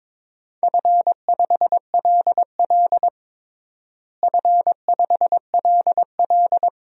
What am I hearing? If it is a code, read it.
Morse code, heard as a keyed tone, about 22 wpm, 700 Hz. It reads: F5LL F5LL
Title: DXCC-100 Call Signs - 4 Characters - Repeated 2x - 22wpm